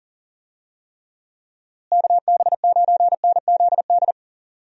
Morse code, heard as a keyed tone, about 40 wpm, 700 Hz. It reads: KB9NZD
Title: USA Call Signs - 6 Characters - Repeated 2x - 40wpm